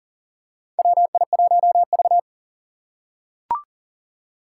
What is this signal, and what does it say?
Morse code, heard as a keyed tone, about 37 wpm, 700 Hz. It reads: WI1V
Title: USA Call Signs - 4 Characters - 3x Character Spacing - 40wpm